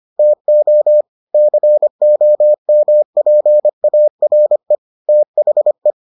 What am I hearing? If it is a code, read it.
Morse code, heard as a keyed tone, about 25 wpm, 600 Hz. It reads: TO COMPARE THE